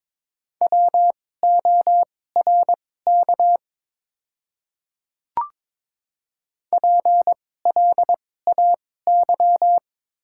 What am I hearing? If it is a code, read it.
Morse code, heard as a keyed tone, about 22 wpm, spaced as for 16 wpm, 700 Hz. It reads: WORK PLAY
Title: 2x Character Spacing - Top 200 Words - 22wpm